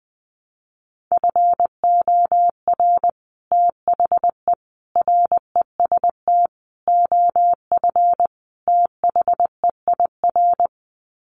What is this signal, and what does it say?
Morse code, heard as a keyed tone, about 20 wpm, 700 Hz. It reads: FOR THE REST OF THEIR
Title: Sets of 5 Words Encore2 20wpm